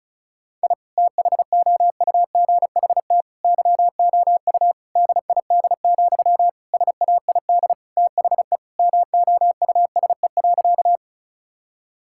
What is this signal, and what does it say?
Morse code, heard as a keyed tone, about 36 wpm, 700 Hz. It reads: I THOUGHT YOU DID, SAID THE MOUSE.
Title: Sentences from Top 1000 Words - Review Part 5 - 35wpm